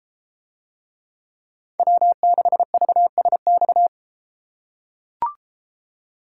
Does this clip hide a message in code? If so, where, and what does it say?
Yes: Morse code from 1.8 to 3.9 s, heard as a keyed tone, about 32 wpm, 700 Hz. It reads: W6VSX